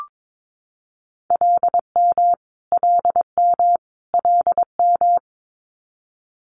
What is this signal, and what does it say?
Morse code, heard as a keyed tone, about 22 wpm, 700 Hz. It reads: LM LM LM
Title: Multi-letter Phonemes Repeated 3x 22wpm